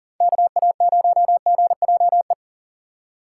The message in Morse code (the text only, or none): KA0GJE